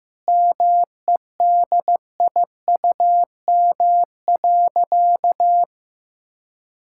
Morse code, heard as a keyed tone, about 16 wpm, 700 Hz. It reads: MEDIUM.
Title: Sentences from Top 1000 Words - Review Part 6 - 15wpm